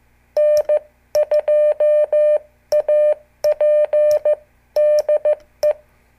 Morse code, heard as a keyed tone, about 14 wpm, 600 Hz. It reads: N2AP DE
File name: morse / qso / CQ